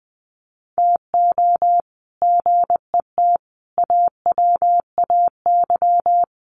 Morse code, heard as a keyed tone, about 19 wpm, 700 Hz. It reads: TO GET AWAY
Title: Sets of 5 Words Encore2 20wpm